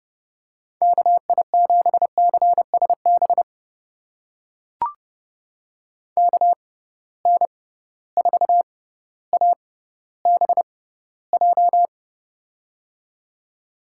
Morse code, heard as a keyed tone, about 30 wpm, spaced as for 15 wpm, 700 Hz. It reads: KI7CSB KN4ABJ
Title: USA Call Signs - 6 Characters - 6x Character Spacing - 30wpm